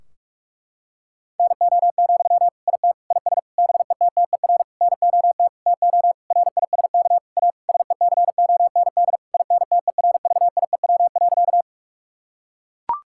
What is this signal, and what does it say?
Morse code, heard as a keyed tone, about 41 wpm, 700 Hz. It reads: NO, IT IS BETTER NOT TO RISK A SECOND INTERVIEW.
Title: Sentences from Top 1600 Words 45wpm